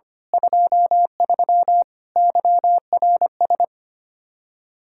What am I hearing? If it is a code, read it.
Morse code, heard as a keyed tone, about 28 wpm, 700 Hz. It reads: 23 YRS
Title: Speed-Racing Common QSO Phrases - 25wpm